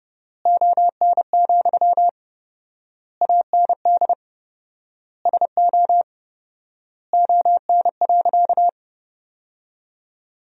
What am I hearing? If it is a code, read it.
Morse code, heard as a keyed tone, about 30 wpm, 700 Hz. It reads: ON, AND SO ON.
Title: 4x Word-Spacing - Sentences from Top 100 Words - 30wpm